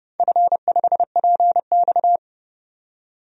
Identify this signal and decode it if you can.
Morse code, heard as a keyed tone, about 31 wpm, 700 Hz. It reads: F5PX
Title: DXCC-100 Call Signs - 4 Characters - 30wpm